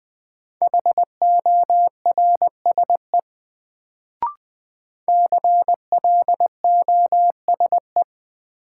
Morse code, heard as a keyed tone, about 20 wpm, 700 Hz. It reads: HORSE CLOSE
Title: Sentences from Top 300 Words - New Word Drill - 20wpm